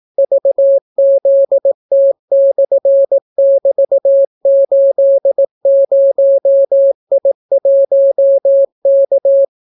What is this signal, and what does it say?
Morse code, heard as a keyed tone, about 18 wpm, 550 Hz. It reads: VZT/<BT>80I1K